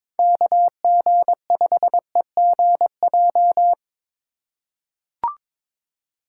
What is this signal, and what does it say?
Morse code, heard as a keyed tone, about 22 wpm, 700 Hz. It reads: KG5EGJ
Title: USA Call Signs - 2x Character Spacing - 22wpm